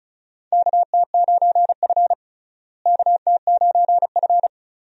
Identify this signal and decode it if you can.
Morse code, heard as a keyed tone, about 36 wpm, 700 Hz. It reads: KT9F KT9F
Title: USA Call Signs - 4 Characters - Repeated 2x - 35wpm